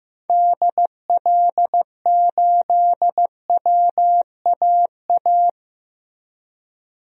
Morse code, heard as a keyed tone, about 15 wpm, 700 Hz. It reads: DL8WAA
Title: DXCC-100 Call Signs - 6 Characters - Repeated 2x - 15wpm